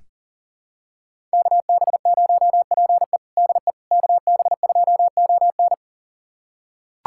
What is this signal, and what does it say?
Morse code, heard as a keyed tone, about 40 wpm, 700 Hz. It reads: KB0PE DE KB2ON
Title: USA Callsign DE Callsign - 5 Characters - 40wpm